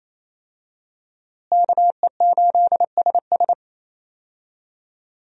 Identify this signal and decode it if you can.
Morse code, heard as a keyed tone, about 28 wpm, 700 Hz. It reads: KE8SS
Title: USA Call Signs - 5 Characters - Repeated 2x - 28wpm